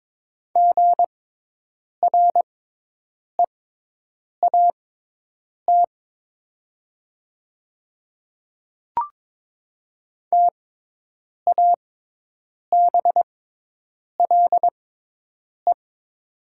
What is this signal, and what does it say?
Morse code, heard as a keyed tone, about 22 wpm, spaced as for 8 wpm, 700 Hz. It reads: GREAT TABLE
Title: 6x Character Spacing - Top 500 Words - 22wpm